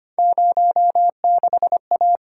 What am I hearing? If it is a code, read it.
Morse code, heard as a keyed tone, about 25 wpm, 700 Hz. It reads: 06A
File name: Three Letter-Number Combinations - 25wpm